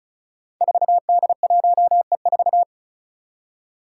Morse code, heard as a keyed tone, about 36 wpm, 700 Hz. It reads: 4D1E4
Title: Five Challenging Letter-Number Combinations 35wpm